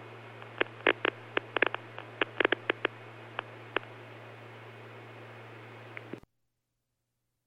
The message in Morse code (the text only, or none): none